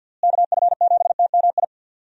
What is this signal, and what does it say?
Morse code, heard as a keyed tone, about 46 wpm, 700 Hz. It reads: KF7TMI
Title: USA Call Signs - 6 Characters - Repeated 3x - 50wpm